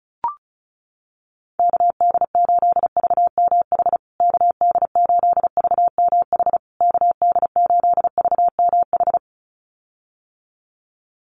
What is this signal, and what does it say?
Morse code, heard as a keyed tone, about 34 wpm, 700 Hz. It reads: KD8VMH KD8VMH KD8VMH
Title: USA Call Signs Repeated 3x 35wpm